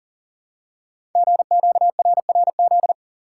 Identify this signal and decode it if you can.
Morse code, heard as a keyed tone, about 38 wpm, 700 Hz. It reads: GQRRZ
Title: Code Groups - 5 Letters - 40wpm